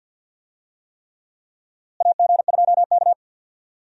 Morse code, heard as a keyed tone, about 52 wpm, 700 Hz. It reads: AG2K
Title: USA Call Signs - 4 Characters - Repeated 3x - 50wpm